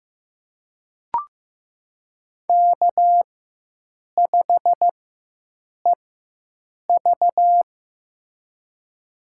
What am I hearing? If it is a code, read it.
Morse code, heard as a keyed tone, about 15 wpm, spaced as for 7 wpm, 700 Hz. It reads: K5EV
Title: USA Call Signs - 4 Characters - 4x Character Spacing - 15wpm